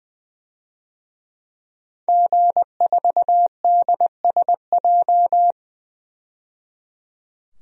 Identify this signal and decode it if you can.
Morse code, heard as a keyed tone, about 20 wpm, 700 Hz. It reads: G4DSJ
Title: DXCC-100 Call Signs - 5 Characters - 20wpm